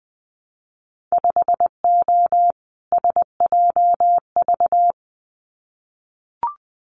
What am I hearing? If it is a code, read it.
Morse code, heard as a keyed tone, about 20 wpm, 700 Hz. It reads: 5O SJV
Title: Winter Field Day Exchanges Encore 20wpm